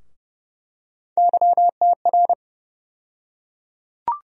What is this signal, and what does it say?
Morse code, heard as a keyed tone, about 29 wpm, 700 Hz. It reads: YTR
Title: USA Call Sign Postfixes - 3 Letters 30wpm